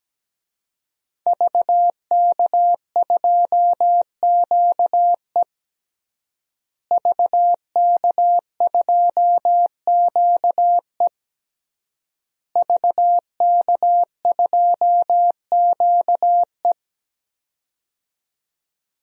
Morse code, heard as a keyed tone, about 17 wpm, 700 Hz. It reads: VK2QE VK2QE VK2QE
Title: DXCC-100 Call Signs - 5 Characters - Repeated 3x - 17wpm